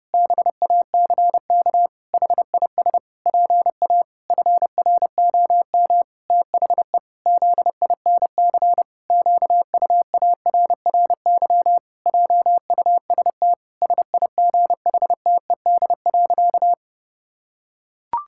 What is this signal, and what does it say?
Morse code, heard as a keyed tone, about 30 wpm, 700 Hz. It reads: BACK HIS PA FROM THE ZINC QUARRY JUST SIGHTED.
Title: Pangram Challenge v2 - 30wpm